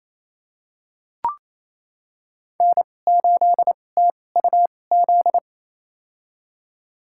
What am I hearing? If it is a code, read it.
Morse code, heard as a keyed tone, about 27 wpm, spaced as for 20 wpm, 700 Hz. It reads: N8TUZ
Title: USA Call Signs - 2x Character Spacing - 28wpm